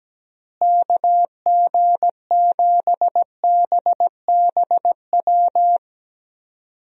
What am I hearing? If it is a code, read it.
Morse code, heard as a keyed tone, about 17 wpm, 700 Hz. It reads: KG7BBW
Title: USA Call Signs - 6 Characters - Repeated 3x - 17wpm